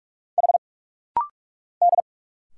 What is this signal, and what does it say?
Morse code, heard as a keyed tone, about 46 wpm, 700 Hz. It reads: H D
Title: Single Letters - Rapid-Fire 45wpm